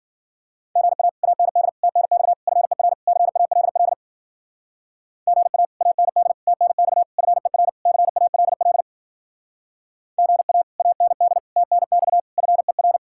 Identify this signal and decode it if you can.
Morse code, heard as a keyed tone, about 50 wpm, 700 Hz. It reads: GA AND TNX FER CALL GA AND TNX FER CALL GA AND TNX FER
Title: Speed-Racing Common QSO Phrases Encore - 40wpm